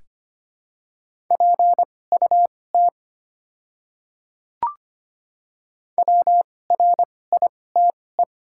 Morse code, heard as a keyed tone, about 25 wpm, spaced as for 18 wpm, 700 Hz. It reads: PUT WRITE
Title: 2x Character Spacing - Top 500 Words - 25wpm